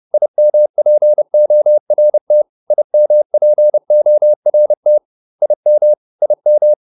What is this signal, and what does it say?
Morse code, heard as a keyed tone, about 30 wpm, 600 Hz. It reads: IMPORT IMPORT IM IM
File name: IM6 30